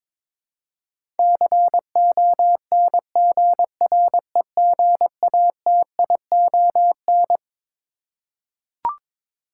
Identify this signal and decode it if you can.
Morse code, heard as a keyed tone, about 21 wpm, 700 Hz. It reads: CONGREGATION
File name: Twelve Letter Words - 22wpm